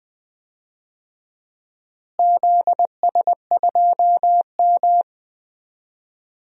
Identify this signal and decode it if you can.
Morse code, heard as a keyed tone, about 20 wpm, 700 Hz. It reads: ZS2M
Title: DXCC-100 Call Signs - 4 Characters - Repeated 3x - 20wpm